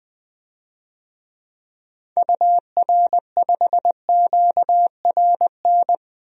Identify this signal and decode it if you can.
Morse code, heard as a keyed tone, about 20 wpm, 700 Hz. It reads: UR5QRN
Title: DXCC-100 Call Signs - 6 Characters - 20wpm